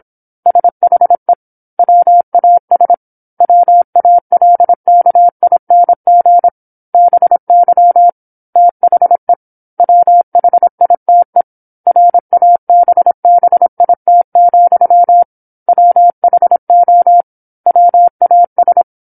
Morse code, heard as a keyed tone, about 26 wpm, 700 Hz. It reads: SHE WAS WALKING BY THE WHITE RABBIT, WHO WAS